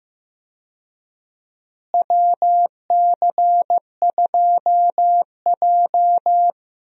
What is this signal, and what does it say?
Morse code, heard as a keyed tone, about 15 wpm, 700 Hz. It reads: WC2J
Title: USA Call Signs - 4 Characters - Repeated 3x - 15wpm